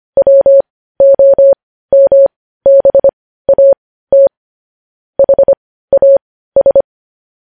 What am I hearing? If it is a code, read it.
Morse code, heard as a keyed tone, about 25 wpm, spaced as for 15 wpm, 550 Hz. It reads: WOMBAT HAS
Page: news headlines in morse code at 15 wpm